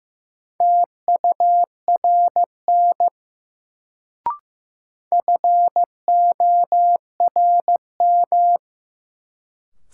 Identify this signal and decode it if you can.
Morse code, heard as a keyed tone, about 15 wpm, 700 Hz. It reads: TURN FORM
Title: Sets of 4 Words - New Word Drill - 15wpm